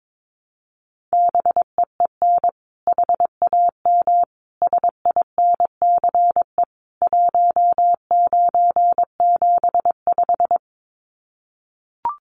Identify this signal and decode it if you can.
Morse code, heard as a keyed tone, about 22 wpm, 700 Hz. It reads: BEEN HAM SINCE 1975
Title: Common QSO Phrases 22wpm